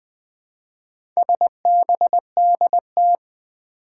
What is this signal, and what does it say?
Morse code, heard as a keyed tone, about 20 wpm, 700 Hz. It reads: SBDT
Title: Four Challenging Letter-Number Combinations 20wpm